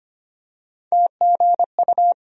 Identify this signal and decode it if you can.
Morse code, heard as a keyed tone, about 25 wpm, 700 Hz. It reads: TGU
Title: USA Call Sign Postfixes - 3 Letters 25wpm